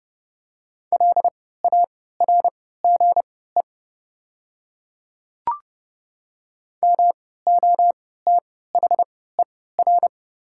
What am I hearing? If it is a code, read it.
Morse code, heard as a keyed tone, about 30 wpm, spaced as for 17 wpm, 690 Hz. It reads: LARGE MOTHER
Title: 3x Character Spacing - Top 200 Words - 30wpm